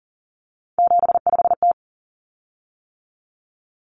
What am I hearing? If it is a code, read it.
Morse code, heard as a keyed tone, about 39 wpm, 700 Hz. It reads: 75T
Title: Common Cut Numbers - 3-Digit 40wpm